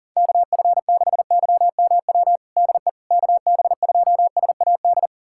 Morse code, heard as a keyed tone, about 40 wpm, 700 Hz. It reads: KF6YMW DE KB2SAD
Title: USA Callsign DE Callsign 40wpm